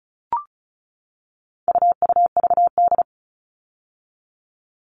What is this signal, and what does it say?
Morse code, heard as a keyed tone, about 34 wpm, 700 Hz. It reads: UUVD
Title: All Cut Numbers - 4-Digit 35wpm